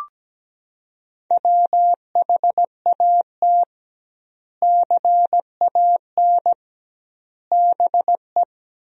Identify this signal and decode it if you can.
Morse code, heard as a keyed tone, about 17 wpm, 700 Hz. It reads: WHAT CAN BE
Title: 2x Word-Spacing - Sentences from Top 500 Words - 17wpm